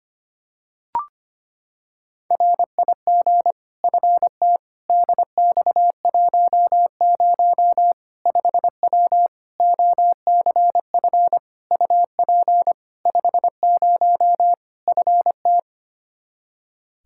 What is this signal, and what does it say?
Morse code, heard as a keyed tone, about 25 wpm, 700 Hz. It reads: RIG FT DX10 5W OCF UP 50 FT
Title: Practice QSO - Exchange 2 - Info - 25wpm